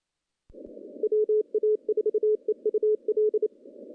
Morse code, heard as a keyed tone, about 25 wpm, 420 Hz. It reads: WA4EUL
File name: contests / CW-SS-k5tr- -0100